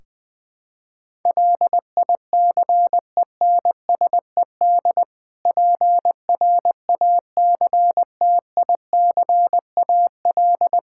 Morse code, heard as a keyed tone, about 20 wpm, 700 Hz. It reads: LICENSED PRACTICAL
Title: Practice QSO - Exchange 3 - More Info - 20wpm